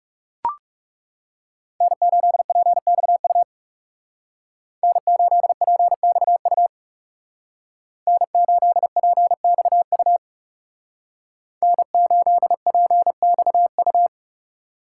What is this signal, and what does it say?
Morse code, heard as a keyed tone, about 39 wpm, 700 Hz. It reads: N8PXU N8PXU N8PXU N8PXU